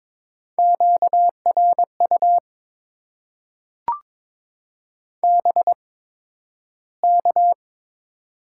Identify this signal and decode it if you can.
Morse code, heard as a keyed tone, about 22 wpm, spaced as for 12 wpm, 700 Hz. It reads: QRU BK
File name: Top 90 QSO Elements - 8x Character Spacing - 22wpm